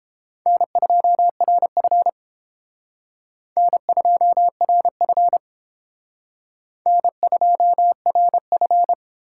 Morse code, heard as a keyed tone, about 32 wpm, 700 Hz. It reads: N2RF N2RF N2RF